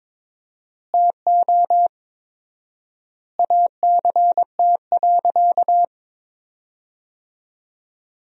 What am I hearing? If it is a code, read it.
Morse code, heard as a keyed tone, about 22 wpm, 700 Hz. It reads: TO ACT.
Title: 4x Word-Spacing - Sentences from Top 200 Words - 22wpm